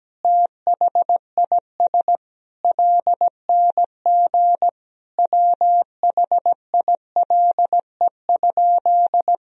Morse code, heard as a keyed tone, about 17 wpm, 700 Hz. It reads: THIS LNG WHILE?
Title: Sentences from Top 300 Words and Abbreviations 17wpm